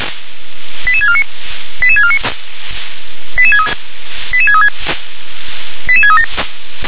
Five-tone selective call (ZVEI), heard as quick runs of five tones, tones of about 70 ms. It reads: 88649 88649 88636 88637
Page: ZVEI